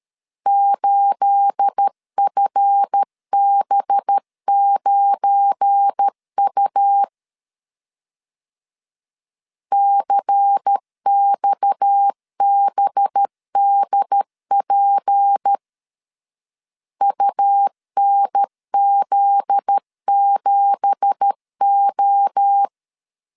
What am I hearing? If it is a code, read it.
Morse code, heard as a keyed tone, about 13 wpm, 800 Hz. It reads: 8FB9U CXBDP UNZ7O
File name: CW / DET / 1 9 1500C S B 12wpm 3819